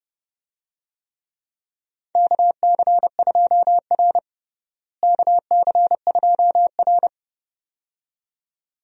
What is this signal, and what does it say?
Morse code, heard as a keyed tone, about 29 wpm, 700 Hz. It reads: KC2R KC2R
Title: USA Call Signs - 4 Characters - Repeated 2x - 30wpm